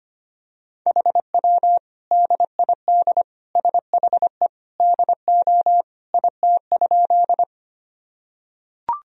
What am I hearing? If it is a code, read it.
Morse code, heard as a keyed tone, about 25 wpm, 700 Hz. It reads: HW DID SHE DO IT?
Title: Sentences from Top 100 Words and Abbreviations 25wpm